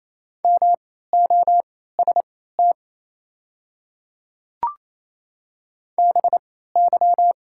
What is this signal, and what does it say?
Morse code, heard as a keyed tone, about 27 wpm, spaced as for 16 wpm, 700 Hz. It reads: MOST BY
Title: 3x Character Spacing - Top 200 Words - 28wpm